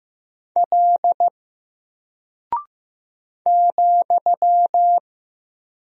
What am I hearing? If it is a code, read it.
Morse code, heard as a keyed tone, about 15 wpm, 700 Hz. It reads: L ,
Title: Single Character (Letter:Number:Punctuation:Prosign) 15wpm